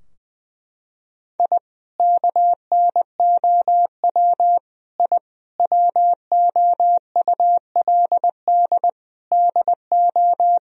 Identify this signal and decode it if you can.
Morse code, heard as a keyed tone, about 20 wpm, 700 Hz. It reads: I KNOW I WOULD DO
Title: Sentences from Top 1000 Words - Review Part 3 - 20wpm